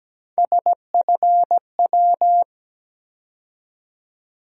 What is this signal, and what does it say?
Morse code, heard as a keyed tone, about 17 wpm, 700 Hz. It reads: SFW
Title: USA Call Sign Postfixes - 3 Letters 17wpm